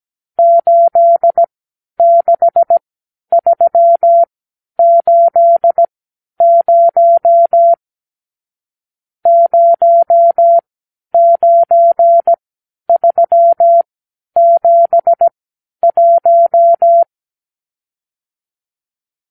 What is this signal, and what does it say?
Morse code, heard as a keyed tone, about 17 wpm, spaced as for 10 wpm, 680 Hz. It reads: 86380 09371
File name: arrl / Review (1-0)